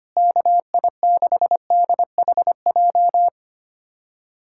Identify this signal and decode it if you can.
Morse code, heard as a keyed tone, about 25 wpm, 700 Hz. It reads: KI6DHJ
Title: USA Callsign DE Callsign 25wpm